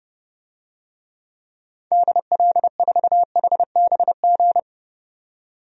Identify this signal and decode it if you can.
Morse code, heard as a keyed tone, about 30 wpm, 700 Hz. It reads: DL4HBG
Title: DXCC-100 Call Signs - 6 Characters - 30wpm